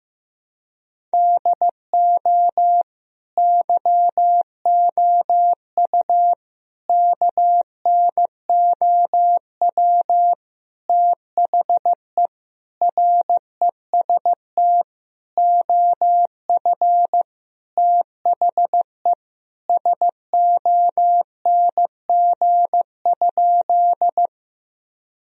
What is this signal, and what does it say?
Morse code, heard as a keyed tone, about 16 wpm, 700 Hz. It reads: DO YOU KNOW THE REST OF THE SONG?
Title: Sentences from Top 1000 Words - Review Part 2 - 15wpm